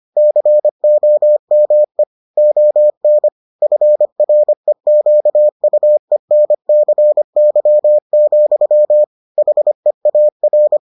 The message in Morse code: COME ON FREQUENCY, HEAR

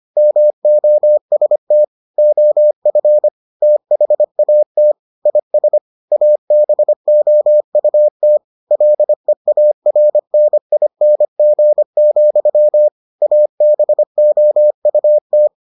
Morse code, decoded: MOST OF THAT IS ABOUT LEARNING, ABOUT

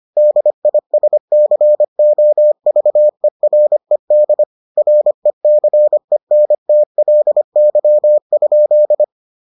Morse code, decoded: DISCOVERED RECENTLY?

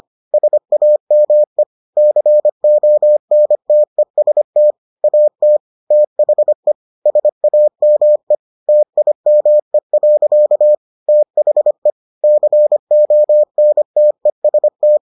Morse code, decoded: SAME CONTEST AT THE SAME TIME. THE CONTEST

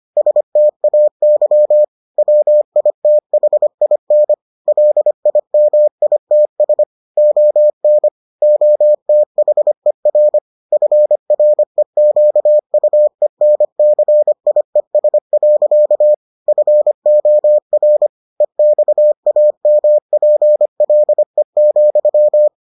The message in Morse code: STAY WITHIN LIMITS ON OTHER FREQUENCIES. FOR EXAMPLE,

